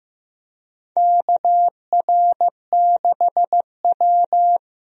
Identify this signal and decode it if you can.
Morse code, heard as a keyed tone, about 16 wpm, 700 Hz. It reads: KR6W
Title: USA Call Signs - 4 Characters - Repeated 2x - 15wpm